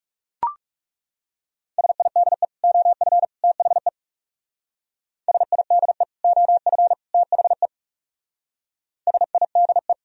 Morse code, heard as a keyed tone, about 36 wpm, 700 Hz. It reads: SIDE OF THE SIDE OF THE SIDE
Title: Speed-Racing - Sets of 3 Words 30wpm